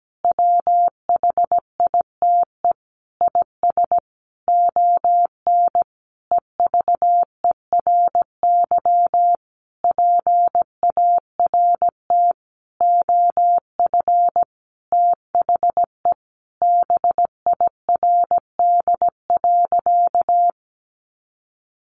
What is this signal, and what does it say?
Morse code, heard as a keyed tone, about 17 wpm, 700 Hz. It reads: WHITE IS ON EVERY PART OF THE BIRD.
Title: Sentences from Top 1500 Words 17wpm